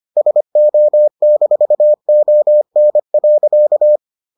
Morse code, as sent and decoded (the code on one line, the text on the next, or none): ... --- -....- --- -. .-.-.-
SO-ON.